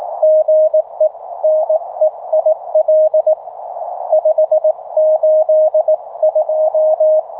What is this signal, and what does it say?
Morse code, heard as a keyed tone, about 22 wpm, spaced as for 19 wpm, 660 Hz. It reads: GE NEIL 582